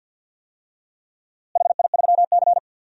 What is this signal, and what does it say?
Morse code, heard as a keyed tone, about 52 wpm, 700 Hz. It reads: HI3C
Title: DXCC-100 Call Signs - 4 Characters - Repeated 2x - 50wpm